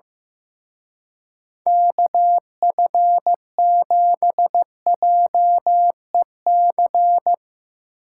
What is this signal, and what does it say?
Morse code, heard as a keyed tone, about 16 wpm, 700 Hz. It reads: KF7JEC
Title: Speed Racing - USA 2x3 Call Signs 15wpm